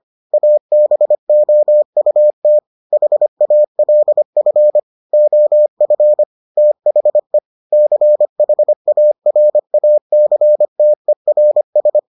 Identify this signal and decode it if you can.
Morse code, heard as a keyed tone, about 25 wpm, 600 Hz. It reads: ABOUT HALF OF THE CHARACTERS